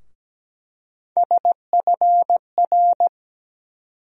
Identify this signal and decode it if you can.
Morse code, heard as a keyed tone, about 17 wpm, 700 Hz. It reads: SFR
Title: USA Call Sign Postfixes - 3 Letters 17wpm